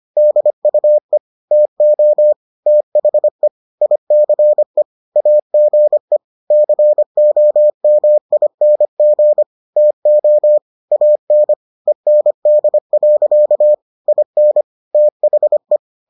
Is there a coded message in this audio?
Yes: Morse code from 0.2 to 15.8 s, heard as a keyed tone, about 25 wpm, 600 Hz. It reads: DUE TO THE ICE AGE COMING TO AN END. IN THE